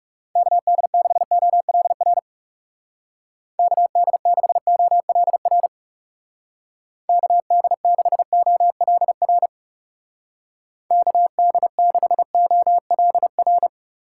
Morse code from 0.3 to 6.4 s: KD6OLR KD6OLR

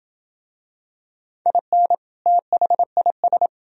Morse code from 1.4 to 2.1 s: IN

